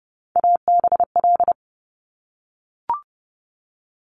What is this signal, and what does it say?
Morse code, heard as a keyed tone, about 31 wpm, 700 Hz. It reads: ABL
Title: Top Three Letter Combinations in Words 30wpm